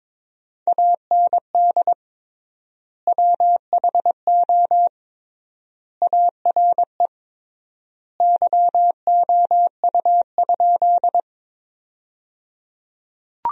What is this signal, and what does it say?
Morse code, heard as a keyed tone, about 21 wpm, 700 Hz. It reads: AND WHO ARE YOU?
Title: 3x Word-Spacing - Sentences from Top 100 Words - 22wpm